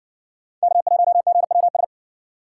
Morse code, heard as a keyed tone, about 60 wpm, 700 Hz. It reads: K1GWS